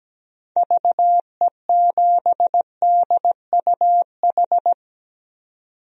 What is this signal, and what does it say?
Morse code, heard as a keyed tone, about 17 wpm, 700 Hz. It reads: VE7DUH
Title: DXCC-100 Call Signs - 6 Characters - 17wpm